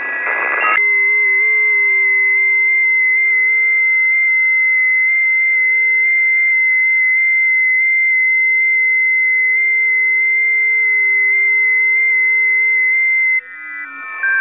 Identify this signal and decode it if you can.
FT8.
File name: tests / 20m busy / test 16